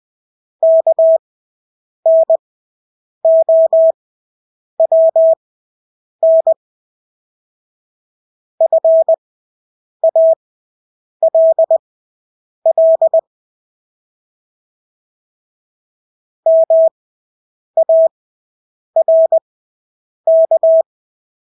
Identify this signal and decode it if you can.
Morse code, heard as a keyed tone, about 20 wpm, spaced as for 8 wpm, 650 Hz. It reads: KNOWN FALL MARK
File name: Engelske ord 20/8 wpm